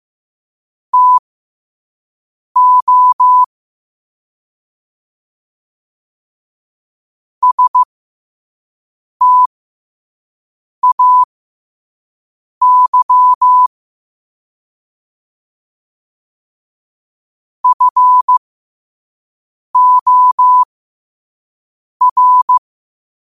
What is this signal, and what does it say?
Morse code, heard as a keyed tone, about 15 wpm, spaced as for 5 wpm, 1000 Hz. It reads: TO STAY FOR
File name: qotdfiles / 5wpm